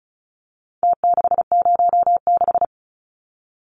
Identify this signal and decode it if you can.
Morse code, heard as a keyed tone, about 34 wpm, 700 Hz. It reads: T606
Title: Numbers - Contest Style - 4-Digit 35wpm